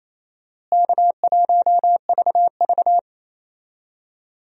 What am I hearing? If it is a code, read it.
Morse code, heard as a keyed tone, about 28 wpm, 700 Hz. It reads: K1VV
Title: USA Call Signs - 4 Characters - Repeated 3x - 28wpm